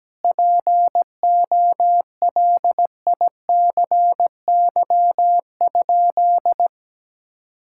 Morse code, heard as a keyed tone, about 17 wpm, 700 Hz. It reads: POLICY?
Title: Sentences from Top 1000 Words - Review Part 5 - 17wpm